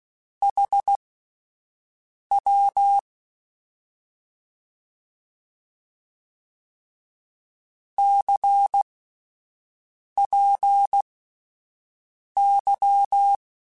Morse code, heard as a keyed tone, about 16 wpm, spaced as for 5 wpm, 800 Hz. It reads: HW CPY
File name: QSOs / QSOs-mp3s / WD0FBT 1@10